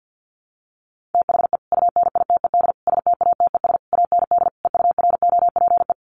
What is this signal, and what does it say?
Morse code, heard as a keyed tone, about 50 wpm, 700 Hz. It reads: THE UNITED STATES AND EUROPE
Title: Sets of 5 Words Encore 50wpm